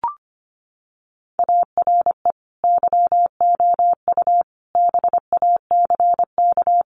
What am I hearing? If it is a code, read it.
Morse code, heard as a keyed tone, about 25 wpm, 700 Hz. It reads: ARE YOU BACK
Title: Sentences from Top 200 Words 25wpm